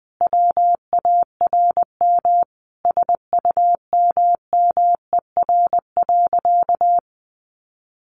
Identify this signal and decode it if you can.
Morse code, heard as a keyed tone, about 20 wpm, 700 Hz. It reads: WARM SUMMER.
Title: Sentences from Top 1100 Words 20wpm